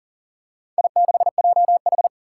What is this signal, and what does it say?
Morse code, heard as a keyed tone, about 40 wpm, 700 Hz. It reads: I6JH
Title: Four Challenging Letter-Number Combinations 40wpm